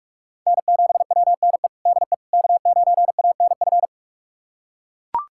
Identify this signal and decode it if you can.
Morse code, heard as a keyed tone, about 41 wpm, 700 Hz. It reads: N7WNE DE K9ANF
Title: USA Callsign DE Callsign - 5 Characters - 45wpm